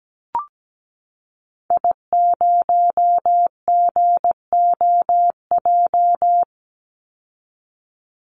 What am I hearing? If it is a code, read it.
Morse code, heard as a keyed tone, about 17 wpm, 700 Hz. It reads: I0GOJ
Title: Call Signs Repeated 2x - 17wpm